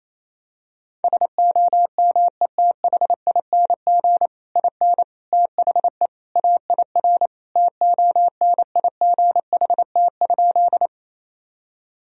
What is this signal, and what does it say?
Morse code, heard as a keyed tone, about 28 wpm, 700 Hz. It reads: SOMETHING IN THE AIR TONIGHT?
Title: Sentences from Top 1000 Words - Review Part 1 - 28wpm